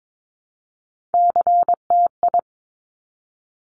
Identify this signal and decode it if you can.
Morse code, heard as a keyed tone, about 22 wpm, 700 Hz. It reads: CTI
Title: Top Three Letter Combinations in Words 22wpm